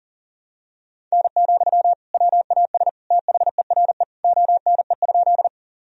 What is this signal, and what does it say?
Morse code, heard as a keyed tone, about 40 wpm, 700 Hz. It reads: N, WAS THERE ONE?
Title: Sentences from Top 100 Words and Abbreviations 40wpm